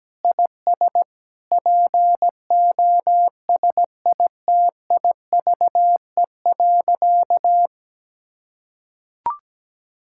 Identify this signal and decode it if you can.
Morse code, heard as a keyed tone, about 17 wpm, 700 Hz. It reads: IS POSITIVE.
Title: Sentences from Top 1000 Words - Review Part 5 - 17wpm